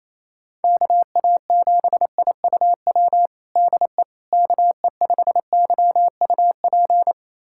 Morse code, heard as a keyed tone, about 28 wpm, 700 Hz. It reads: KA7IUW DE KE5YUP
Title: USA Callsign DE Callsign - 6 Characters - 28wpm